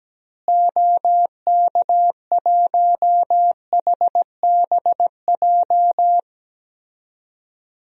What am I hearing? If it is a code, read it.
Morse code, heard as a keyed tone, about 17 wpm, 700 Hz. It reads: OK1HBJ
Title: DXCC-100 Call Signs - 6 Characters - Repeated 3x - 17wpm